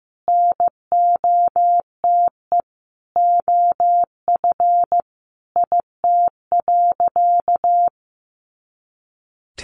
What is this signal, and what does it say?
Morse code, heard as a keyed tone, about 15 wpm, 700 Hz. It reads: NOTE OF IT.
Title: Sentences from Top 500 Words 15wpm